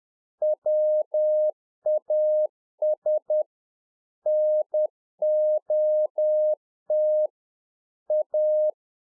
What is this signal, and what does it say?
Morse code, heard as a keyed tone, about 10 wpm, 610 Hz. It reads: WAS NOT A